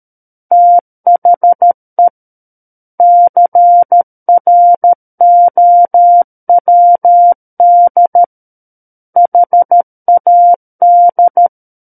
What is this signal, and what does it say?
Morse code, heard as a keyed tone, about 13 wpm, 700 Hz. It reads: THE CROWD HAD
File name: Book One / 04-Chapter 4 - 13 WPM